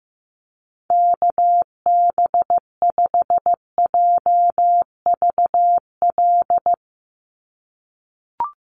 Morse code, heard as a keyed tone, about 15 wpm, 700 Hz. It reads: KB5JVL
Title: USA Call Signs 15wpm